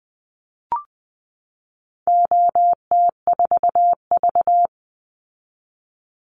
Morse code, heard as a keyed tone, about 19 wpm, 700 Hz. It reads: OT4V
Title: Call Signs - 20wpm